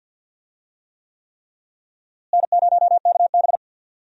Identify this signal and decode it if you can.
Morse code, heard as a keyed tone, about 52 wpm, 700 Hz. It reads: N0KB
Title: USA Call Signs - 4 Characters - Repeated 3x - 50wpm